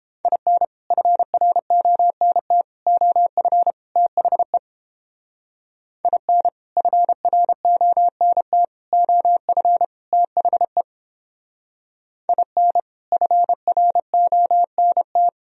Morse code, heard as a keyed tone, about 29 wpm, 700 Hz. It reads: IN FRONT OF THE IN FRONT OF THE IN FRONT